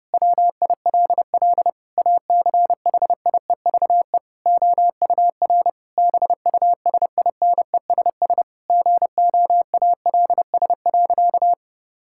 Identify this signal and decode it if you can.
Morse code, heard as a keyed tone, about 28 wpm, 700 Hz. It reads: WILL ACHIEVE OUR BUSINESS GOALS.
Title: Sentences from Top 1000 Words - Review Part 6 - 30wpm